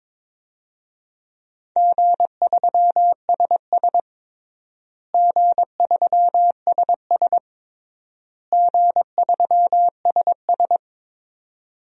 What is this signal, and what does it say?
Morse code, heard as a keyed tone, about 23 wpm, 700 Hz. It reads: G3SS G3SS G3SS
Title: DXCC-100 Call Signs - 4 Characters - Repeated 3x - 22wpm